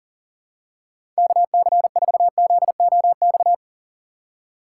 Morse code, heard as a keyed tone, about 40 wpm, 700 Hz. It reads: KC4ZOX
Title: USA Call Signs - 6 Characters - 3x Character Spacing - 40wpm